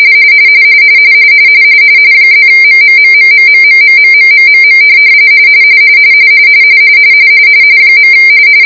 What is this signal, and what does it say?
RTTY, about 45.45 baud. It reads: +MMMMMMMMMMMMM .